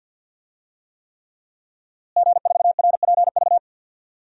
Morse code, heard as a keyed tone, about 52 wpm, 700 Hz. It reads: G4RPV
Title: DXCC-100 Call Signs - 5 Characters - Repeated 3x - 50wpm